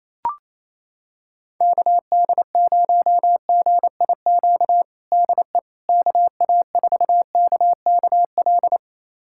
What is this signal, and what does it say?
Morse code, heard as a keyed tone, about 28 wpm, 700 Hz. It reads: KD0GIQ DE KA4KKL
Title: USA Callsign DE Callsign - 6 Characters - 28wpm